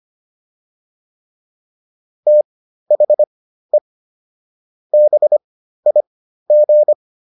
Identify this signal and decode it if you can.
Morse code, heard as a keyed tone, about 25 wpm, spaced as for 13 wpm, 600 Hz. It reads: THE BIG